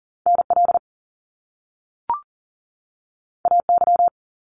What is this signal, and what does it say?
Morse code, heard as a keyed tone, about 41 wpm, 700 Hz. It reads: NL AY
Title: Top Two Letter Combinations in Words 40wpm